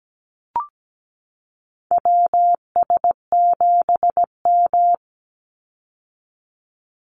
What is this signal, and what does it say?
Morse code, heard as a keyed tone, about 17 wpm, 700 Hz. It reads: WS7M
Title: Call Signs - 17wpm